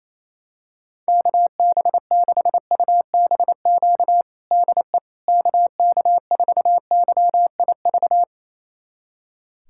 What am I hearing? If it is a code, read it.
Morse code, heard as a keyed tone, about 28 wpm, 700 Hz. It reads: KB6UBQ DE KK4YIV
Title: USA Callsign DE Callsign - 6 Characters - 28wpm